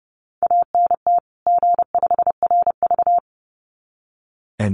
Morse code, heard as a keyed tone, about 30 wpm, 700 Hz. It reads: ANT G5RV
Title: Common QSO Phrases 30wpm